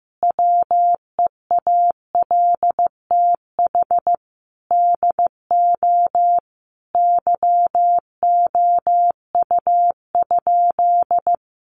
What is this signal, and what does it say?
Morse code, heard as a keyed tone, about 15 wpm, 700 Hz. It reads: WEALTH DO YOU?
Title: Sentences from Top 1200 Words 15wpm